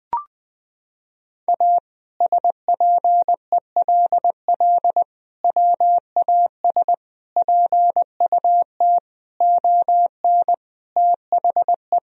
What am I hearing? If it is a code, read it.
Morse code, heard as a keyed tone, about 20 wpm, 700 Hz. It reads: A SPELL WAS PUT ON THE
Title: Sentences from Top 1000 Words - Review Part 1 - 20wpm